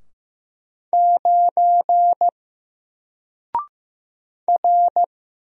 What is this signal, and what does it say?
Morse code, heard as a keyed tone, about 15 wpm, 700 Hz. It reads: 9 R
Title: Single Character (Letter:Number:Punctuation:Prosign) 15wpm